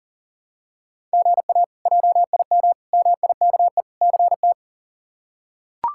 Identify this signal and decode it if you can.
Morse code, heard as a keyed tone, about 40 wpm, 700 Hz. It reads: GA JIM MIKE CT
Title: K1USN SST QSO Practice - 40wpm